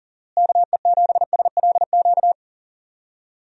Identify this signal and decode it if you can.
Morse code, heard as a keyed tone, about 39 wpm, 700 Hz. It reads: KE7SLQ
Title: USA Call Signs - 6 Characters - Repeated 3x - 40wpm